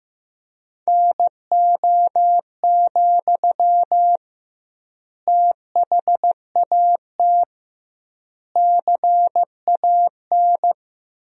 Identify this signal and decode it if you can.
Morse code, heard as a keyed tone, about 15 wpm, 700 Hz. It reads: NO, THAT CAN
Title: 2x Word-Spacing - Sentences from Top 300 Words - 15wpm